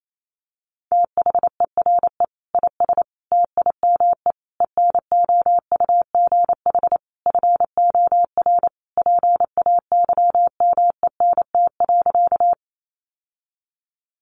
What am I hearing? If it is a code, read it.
Morse code, heard as a keyed tone, about 28 wpm, 700 Hz. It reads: THERE IS TIME ENOUGH FOR PAYMENT.
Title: Sentences from Top 1200 Words 28wpm